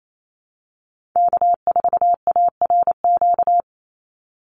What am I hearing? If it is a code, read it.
Morse code, heard as a keyed tone, about 28 wpm, 690 Hz. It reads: K4ARQ
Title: Call Signs Repeated 3x - 28wpm